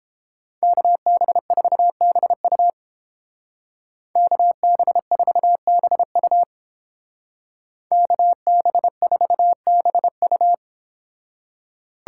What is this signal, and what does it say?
Morse code, heard as a keyed tone, about 30 wpm, 700 Hz. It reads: KB4BU KB4BU KB4BU